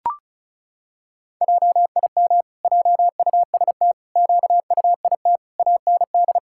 Morse code, heard as a keyed tone, about 34 wpm, 700 Hz. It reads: JIM JUST QUIT AND
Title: Pangram Challenge v2 - 35wpm